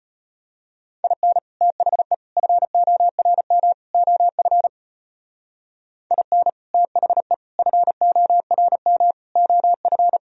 Morse code, heard as a keyed tone, about 31 wpm, 700 Hz. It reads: IN THE FORM OF IN THE FORM OF
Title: Speed-Racing - Sets of 4 Words 25wpm